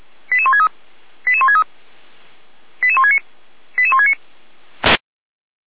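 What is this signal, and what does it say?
Five-tone selective call (ZVEI), heard as quick runs of five tones, tones of about 70 ms. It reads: 88163 88163 88169 88169
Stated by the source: ZVEI five-tone selective call received on SDR.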